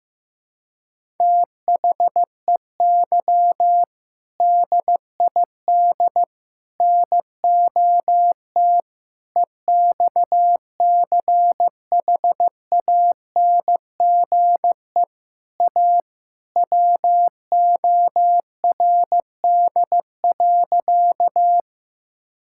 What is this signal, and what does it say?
Morse code, heard as a keyed tone, about 15 wpm, 700 Hz. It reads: THEY DID NOT EXCHANGE A WORD.
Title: Sentences from Top 1000 Words - Review Part 6 - 15wpm